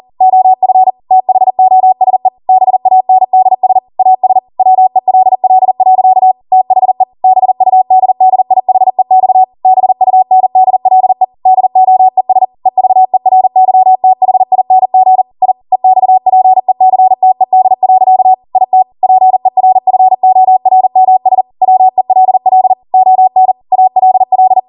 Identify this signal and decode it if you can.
Morse code, heard as a keyed tone, about 40 wpm, 750 Hz. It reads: OF THOSE BANDS AS WELL. THE BUDDIHEX BUNDLE DOES EVERYTHING I EXPECTED. IT PERFORMS WELL ON ALL
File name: Morse / 40WPM